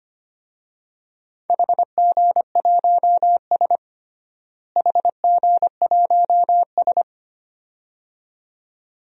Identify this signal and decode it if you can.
Morse code, heard as a keyed tone, about 25 wpm, 700 Hz. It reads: HG1S HG1S
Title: DXCC-100 Call Signs - 4 Characters - Repeated 2x - 25wpm